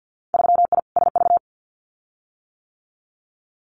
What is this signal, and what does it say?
Morse code, heard as a keyed tone, about 50 wpm, 700 Hz. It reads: 3I SV
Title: Winter Field Day Exchanges Encore 50wpm